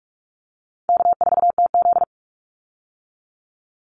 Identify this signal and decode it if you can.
Morse code, heard as a keyed tone, about 44 wpm, 700 Hz. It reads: K4TZ
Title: Easy Call Signs - 45wpm